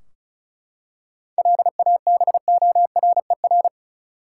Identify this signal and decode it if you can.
Morse code, heard as a keyed tone, about 34 wpm, 700 Hz. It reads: LABORER
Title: K1USN SST QSO Practice 35wpm